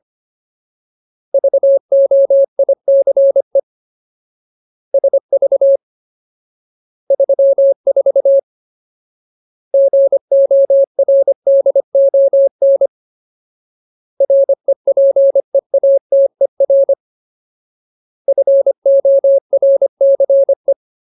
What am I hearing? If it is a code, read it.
Morse code, heard as a keyed tone, about 25 wpm, 550 Hz. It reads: VOICE SV 34 GORDON REPEATER FORCE
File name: Ultimate Word List / 25WPM Ultimate